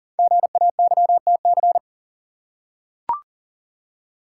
Morse code, heard as a keyed tone, about 39 wpm, 700 Hz. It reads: GAYTC
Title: Code Groups - 5 Letters - 40wpm